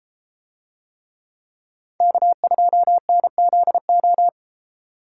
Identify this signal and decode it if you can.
Morse code, heard as a keyed tone, about 32 wpm, 700 Hz. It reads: K2NZO